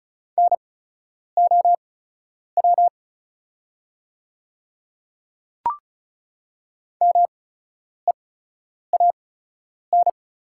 Morse code, heard as a keyed tone, about 36 wpm, spaced as for 10 wpm, 690 Hz. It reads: NOW MEAN
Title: 8x Character Spacing - Top 200 Words - 35wpm